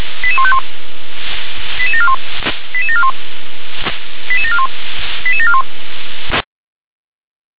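Five-tone selective call (ZVEI), heard as quick runs of five tones, tones of about 70 ms. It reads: 88151 88741 88741 88741 88741
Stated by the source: ZVEI five-tone selective call 88226 received with an SDR.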